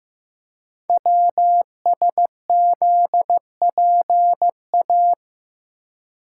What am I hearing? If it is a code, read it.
Morse code, heard as a keyed tone, about 15 wpm, 700 Hz. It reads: WSZPA
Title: Code Groups - 5 Letters - 15wpm